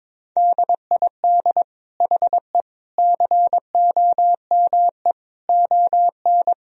Morse code, heard as a keyed tone, about 21 wpm, 700 Hz. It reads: DID HE COME ON
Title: Sentences from Top 1000 Words - Review Part 4 - 22wpm